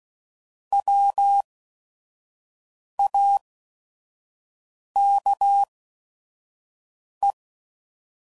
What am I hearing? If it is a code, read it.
Morse code, heard as a keyed tone, about 16 wpm, spaced as for 5 wpm, 800 Hz. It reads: WAKE
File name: Prefixes / By First Letter / Ws